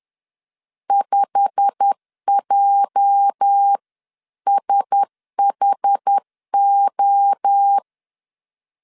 Morse code, heard as a keyed tone, about 10 wpm, 800 Hz. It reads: 5JSHO
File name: CW / DET / 1 9 1500C S A 10wpm 4511